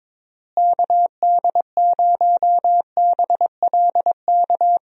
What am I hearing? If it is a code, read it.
Morse code, heard as a keyed tone, about 22 wpm, 700 Hz. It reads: KD0BLK
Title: USA Call Signs - 6 Characters - Repeated 3x - 22wpm